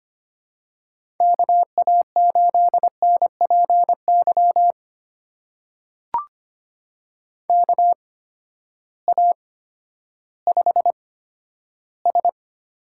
Morse code, heard as a keyed tone, about 25 wpm, spaced as for 12 wpm, 700 Hz. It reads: KA8NPY KA5S
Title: USA Call Signs - 6 Characters - 8x Character Spacing - 25wpm